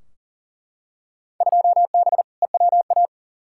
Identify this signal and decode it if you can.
Morse code, heard as a keyed tone, about 40 wpm, 700 Hz. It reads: 2B EWA
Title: Field Day Exchanges Encore - 40wpm